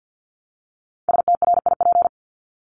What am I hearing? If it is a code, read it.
Morse code, heard as a keyed tone, about 50 wpm, 700 Hz. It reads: STRIP
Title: Top 1501-1750 Words 50wpm